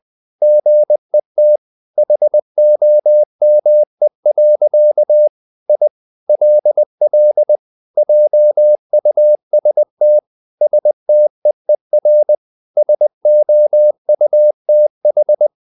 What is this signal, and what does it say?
Morse code, heard as a keyed tone, about 20 wpm, 600 Hz. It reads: GET HOME. I LL JUST STEER SOUTH
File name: the old man and the sea part 2 20wpm